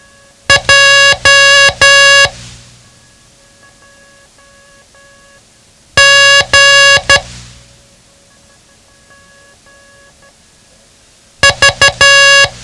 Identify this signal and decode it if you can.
Morse code, heard as a keyed tone, about 12 wpm, 1700 Hz. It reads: J G V
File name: ham radio / CodeQuick / Code Quick 07